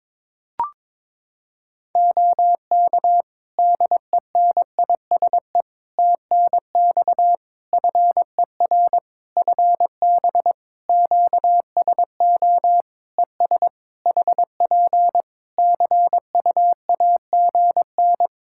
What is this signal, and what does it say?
Morse code, heard as a keyed tone, about 22 wpm, 700 Hz. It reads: OK DENISE TNX FER FB QSO ES HP CUAGN
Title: K1USN SST QSO Practice 22wpm